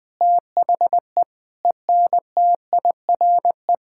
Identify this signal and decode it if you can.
Morse code, heard as a keyed tone, about 20 wpm, 700 Hz. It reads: THE ENTIRE
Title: Sentences from Top 1000 Words - Review Part 5 - 20wpm